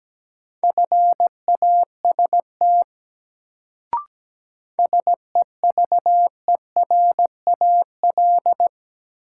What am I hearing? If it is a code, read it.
Morse code, heard as a keyed tone, about 17 wpm, 700 Hz. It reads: FAST SEVERAL
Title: Sentences from Top 400 Words - New Word Drill - 17wpm